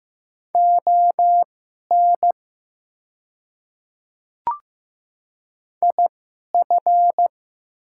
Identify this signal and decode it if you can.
Morse code, heard as a keyed tone, about 15 wpm, spaced as for 11 wpm, 700 Hz. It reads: ON IF
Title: 2x Character Spacing - Top 200 Words - 15wpm